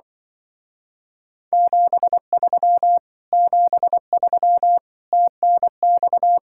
Morse code, heard as a keyed tone, about 24 wpm, 700 Hz. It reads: 73 73 TNX
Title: Speed-Racing Common QSO Phrases Encore - 20wpm